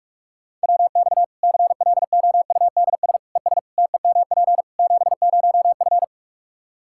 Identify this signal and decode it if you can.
Morse code, heard as a keyed tone, about 46 wpm, 700 Hz. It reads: WX CLOUDS ES TEMP 70F
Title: Practice QSO - Exchange 2 - Info - 45wpm